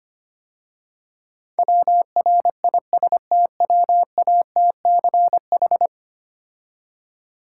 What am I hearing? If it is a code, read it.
Morse code, heard as a keyed tone, about 28 wpm, 700 Hz. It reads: WRISTWATCH
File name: Easy Words - Speed-Racing with Hints - 25wpm